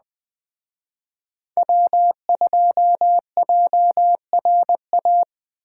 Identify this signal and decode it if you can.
Morse code, heard as a keyed tone, about 20 wpm, 700 Hz. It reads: W2JRA